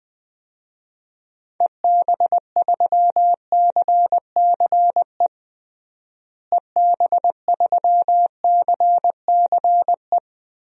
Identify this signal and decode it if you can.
Morse code, heard as a keyed tone, about 21 wpm, 700 Hz. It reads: EB3CCE EB3CCE
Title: DXCC-100 Call Signs - 6 Characters - Repeated 3x - 20wpm